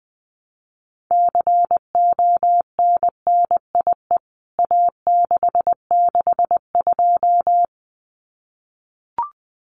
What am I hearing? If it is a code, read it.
Morse code, heard as a keyed tone, about 20 wpm, 700 Hz. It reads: CONNIE A662
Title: CWT Exchange 20wpm